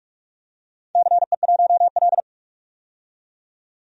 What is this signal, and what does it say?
Morse code, heard as a keyed tone, about 46 wpm, 700 Hz. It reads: CE1L
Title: DXCC-100 Call Signs - 4 Characters - 45wpm